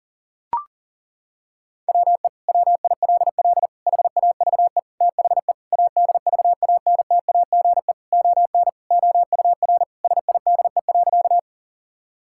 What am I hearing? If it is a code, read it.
Morse code, heard as a keyed tone, about 40 wpm, 700 Hz. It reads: WE WILL HAVE THE ADVANTAGE ON OUR SIDE.
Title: Sentences from Top 1000 Words - Review Part 6 - 40wpm